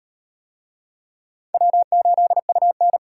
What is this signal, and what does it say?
Morse code, heard as a keyed tone, about 34 wpm, 700 Hz. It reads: W8UN